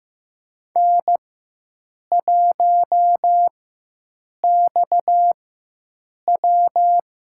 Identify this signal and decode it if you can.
Morse code, heard as a keyed tone, about 15 wpm, spaced as for 7 wpm, 700 Hz. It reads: N1XW
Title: USA Call Signs - 4 Characters - 4x Character Spacing - 15wpm